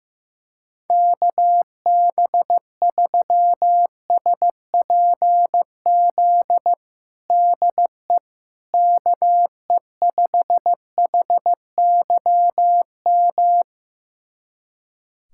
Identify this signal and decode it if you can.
Morse code, heard as a keyed tone, about 15 wpm, 700 Hz. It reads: KB3SPZ DE KE5HYM
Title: USA Callsign DE Callsign - 6 Characters - 15wpm